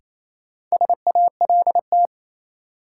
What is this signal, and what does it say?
Morse code, heard as a keyed tone, about 27 wpm, 700 Hz. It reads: SALT
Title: Sets of 5 Words - New Word Drill - 28wpm